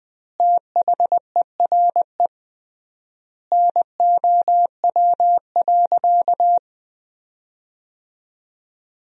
Morse code, heard as a keyed tone, about 20 wpm, 700 Hz. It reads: THERE NOW.
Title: 3x Word-Spacing - Sentences from Top 200 Words - 20wpm